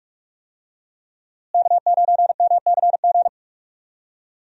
Morse code, heard as a keyed tone, about 43 wpm, 700 Hz. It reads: K9MCG
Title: USA Call Signs - 5 Characters - Repeated 2x - 45wpm